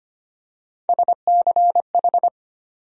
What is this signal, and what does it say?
Morse code, heard as a keyed tone, about 25 wpm, 700 Hz. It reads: SCH
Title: USA Call Sign Postfixes - 3 Letters 25wpm